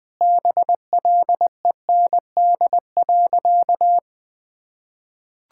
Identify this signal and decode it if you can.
Morse code, heard as a keyed tone, about 20 wpm, 690 Hz. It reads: BLEND.
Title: Sentences with 4-Word Phrases - Spoken Beginning - 20wpm